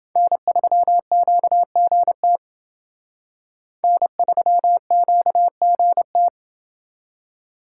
Morse code, heard as a keyed tone, about 28 wpm, 700 Hz. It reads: N3QGT N3QGT